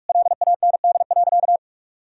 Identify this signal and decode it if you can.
Morse code, heard as a keyed tone, about 46 wpm, 700 Hz. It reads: LAND.
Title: Sentences from Top 1000 Words - Review Part 1 - 45wpm